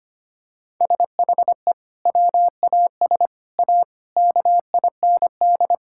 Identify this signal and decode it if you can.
Morse code, heard as a keyed tone, about 25 wpm, 700 Hz. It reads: SHE WAS A KIND
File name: practice / 25wpm / Sentences with 3-Word Phrases - 25wpm